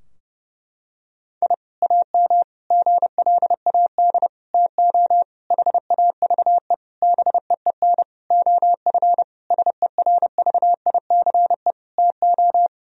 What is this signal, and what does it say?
Morse code, heard as a keyed tone, about 30 wpm, 700 Hz. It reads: I AM GLAD TO HAVE BEEN OF SERVICE TO
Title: Sentences from Top 1000 Words - Review Part 4 - 30wpm